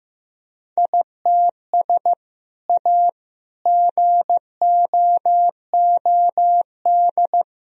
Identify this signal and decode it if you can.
Morse code, heard as a keyed tone, about 15 wpm, 700 Hz. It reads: ITS A GOOD
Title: Sentences from Top 1000 Words - Review Part 1 - 15wpm